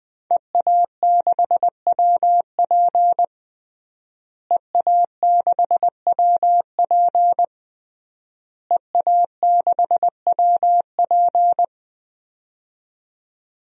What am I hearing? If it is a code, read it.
Morse code, heard as a keyed tone, about 20 wpm, 700 Hz. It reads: EA6WP EA6WP EA6WP
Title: DXCC-100 Call Signs - 5 Characters - Repeated 3x - 20wpm